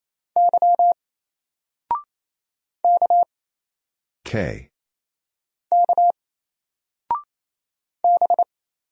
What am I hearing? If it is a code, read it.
Morse code, heard as a keyed tone, about 27 wpm, 700 Hz. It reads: Y K K B
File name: Single-Letter-Number-28wpm